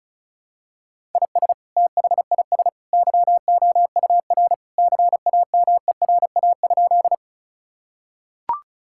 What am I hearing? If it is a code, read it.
Morse code, heard as a keyed tone, about 34 wpm, 700 Hz. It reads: IS THIS YOUR CAMERA?
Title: Sentences from Top 1000 Words - Review Part 5 - 35wpm